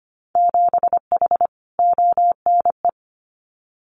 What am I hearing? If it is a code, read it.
Morse code, heard as a keyed tone, about 25 wpm, 700 Hz. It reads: 7H ONE
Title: Winter Field Day Exchanges Encore 25wpm